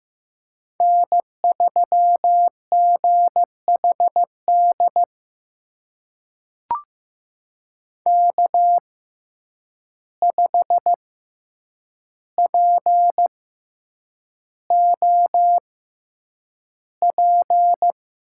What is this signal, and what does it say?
Morse code, heard as a keyed tone, about 15 wpm, spaced as for 8 wpm, 700 Hz. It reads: N3GHD K5POP
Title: USA Call Signs - 5 Characters - 6x Character Spacing - 15wpm